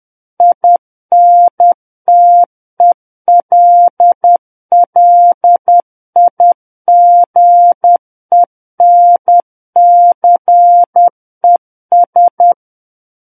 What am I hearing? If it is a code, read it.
Morse code, heard as a keyed tone, about 10 wpm, 700 Hz. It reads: INTELLIGENCES
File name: Book One / 01-Chapter 1 - 10 WPM